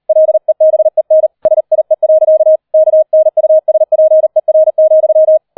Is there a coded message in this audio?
Yes: Morse code from 0.1 to 5.4 s, heard as a keyed tone, about 38 wpm, 630 Hz. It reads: LEBEN SIE. KNUSPER,